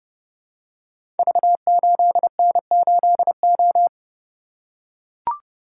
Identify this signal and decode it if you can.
Morse code, heard as a keyed tone, about 30 wpm, 700 Hz. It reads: V8N8O
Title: Code Groups - 5 Letter:Numbers - 30wpm